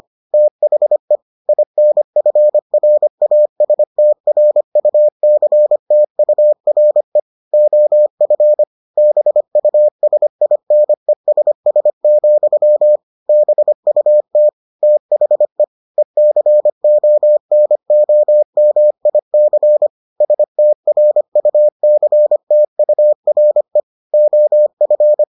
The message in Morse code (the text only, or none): THE INFRASTRUCTURE OF BUSINESS, BUT THE ECONOMIC STRUCTURE OF